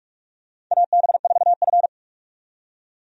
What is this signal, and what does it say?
Morse code, heard as a keyed tone, about 46 wpm, 700 Hz. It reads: AB4F
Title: USA Call Signs - 4 Characters - 45wpm